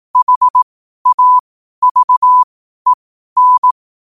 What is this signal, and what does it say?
Morse code, heard as a keyed tone, about 18 wpm, spaced as for 12 wpm, 1000 Hz. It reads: HAVEN
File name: qotdfiles / 12wpm